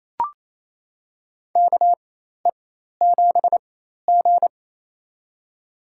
Morse code, heard as a keyed tone, about 28 wpm, spaced as for 13 wpm, 700 Hz. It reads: KE7G
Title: USA Call Signs - 4 Characters - 4x Character Spacing - 28wpm